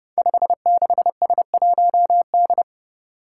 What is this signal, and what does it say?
Morse code, heard as a keyed tone, about 30 wpm, 700 Hz. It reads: 56S1D
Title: Five Challenging Letter-Number Combinations 30wpm